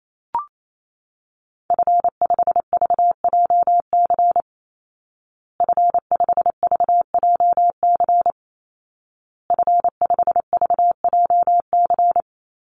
Morse code, heard as a keyed tone, about 28 wpm, 700 Hz. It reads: F5VJC F5VJC F5VJC
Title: Call Signs Repeated 2x - 28wpm